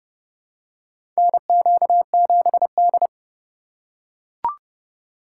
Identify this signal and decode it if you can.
Morse code, heard as a keyed tone, about 30 wpm, 700 Hz. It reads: NQ7D
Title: USA Call Signs - 4 Characters - 2x Character Spacing - 30wpm